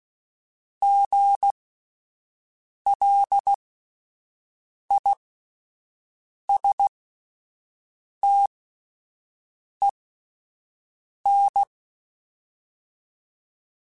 Morse code, heard as a keyed tone, about 16 wpm, spaced as for 5 wpm, 800 Hz. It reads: GLISTEN